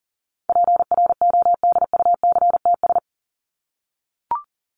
Morse code, heard as a keyed tone, about 40 wpm, 700 Hz. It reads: PRODUCTS
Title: ES:S Suffix - Top 100 Words with Base Words 40wpm